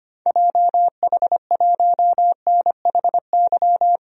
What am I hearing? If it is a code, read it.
Morse code, heard as a keyed tone, about 25 wpm, 700 Hz. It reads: JH1NHY
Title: DXCC-100 Call Signs - 6 Characters - 25wpm